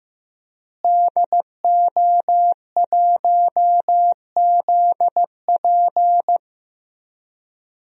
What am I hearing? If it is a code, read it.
Morse code, heard as a keyed tone, about 15 wpm, 700 Hz. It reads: DO1ZP
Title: DXCC-100 Call Signs - 5 Characters - Repeated 3x - 15wpm